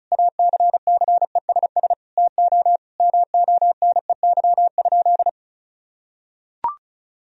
Morse code, heard as a keyed tone, about 34 wpm, 700 Hz. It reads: ACCESS TO MONEY?
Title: Sentences from Top 1000 Words - Review Part 4 - 35wpm